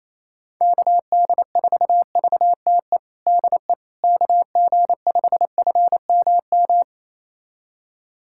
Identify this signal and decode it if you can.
Morse code, heard as a keyed tone, about 28 wpm, 700 Hz. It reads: KD4VTE DE KG5FMM
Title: USA Callsign DE Callsign - 6 Characters - 28wpm